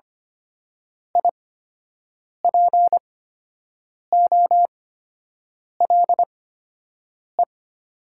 Morse code, heard as a keyed tone, about 25 wpm, spaced as for 7 wpm, 700 Hz. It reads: IPOLE
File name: Top 90 QSO Elements - 8x Character Spacing - 25wpm